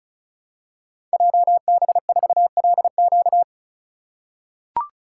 Morse code, heard as a keyed tone, about 36 wpm, 700 Hz. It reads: JB4LQ
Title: Code Groups - 5 Letter:Numbers - 35wpm